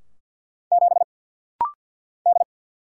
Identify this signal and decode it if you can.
Morse code, heard as a keyed tone, about 50 wpm, 700 Hz. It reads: 7 D
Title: Single Letter-Number - Rapid-Fire 50wpm